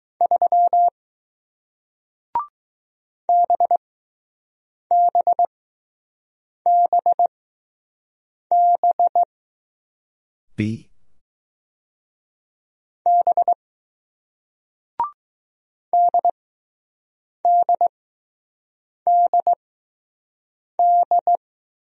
Morse code, heard as a keyed tone, about 20 wpm, spaced as for 6 wpm, 700 Hz. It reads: 3 BBBB B DDDD